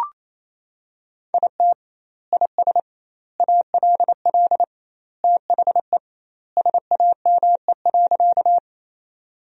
Morse code, heard as a keyed tone, about 28 wpm, 700 Hz. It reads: IT IS ALL THE SAME.
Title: 2x Word-Spacing - Sentences from Top 200 Words - 28wpm